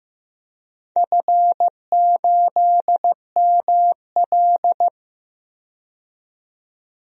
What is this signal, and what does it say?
Morse code, heard as a keyed tone, about 15 wpm, 700 Hz. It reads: F8ML
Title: DXCC-100 Call Signs - 4 Characters - 15wpm